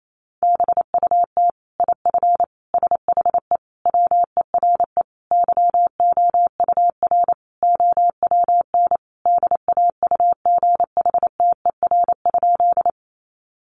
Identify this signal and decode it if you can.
Morse code, heard as a keyed tone, about 28 wpm, 700 Hz. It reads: BUT IF SHE WERE YOUR OWN DAUGHTER?
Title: Sentences from Top 1500 Words 28wpm